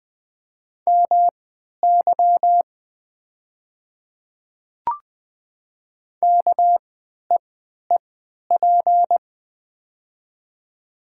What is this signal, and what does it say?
Morse code, heard as a keyed tone, about 20 wpm, spaced as for 11 wpm, 700 Hz. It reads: MY KEEP
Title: 3x Character Spacing - Top 500 Words - 20wpm